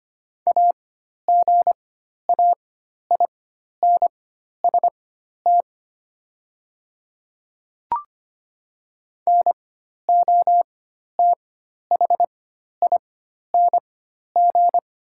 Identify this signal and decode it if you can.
Morse code, heard as a keyed tone, about 25 wpm, spaced as for 12 wpm, 700 Hz. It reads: AGAINST NOTHING
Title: 4x Character Spacing - Top 500 Words - 25wpm